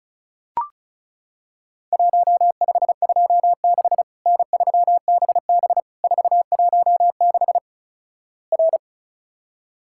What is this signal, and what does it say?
Morse code, heard as a keyed tone, about 36 wpm, 700 Hz. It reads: 1526 N3BB 416 R
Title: Worked All Europe DX Contest - QTC Exchange - 35wpm